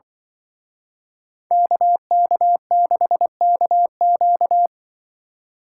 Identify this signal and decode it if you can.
Morse code, heard as a keyed tone, about 25 wpm, 700 Hz. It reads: KK6KQ
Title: Speed Racing - USA 2x2 Call Signs 20wpm